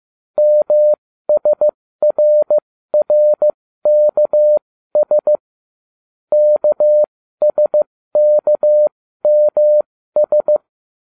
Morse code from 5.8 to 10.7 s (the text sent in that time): KSKMS